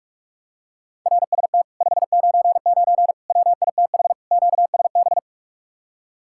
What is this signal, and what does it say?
Morse code, heard as a keyed tone, about 50 wpm, 700 Hz. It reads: RST 599 WITH QSB